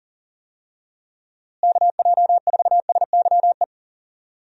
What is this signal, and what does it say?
Morse code, heard as a keyed tone, about 40 wpm, 700 Hz. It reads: KJ4SYE
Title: USA Call Signs - 6 Characters - Repeated 2x - 40wpm